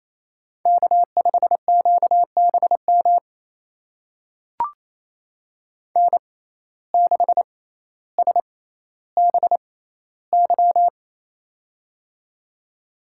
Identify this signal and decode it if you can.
Morse code, heard as a keyed tone, about 28 wpm, spaced as for 14 wpm, 700 Hz. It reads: K5QBM N6SBY
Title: USA Call Signs - 5 Characters - 6x Character Spacing - 28wpm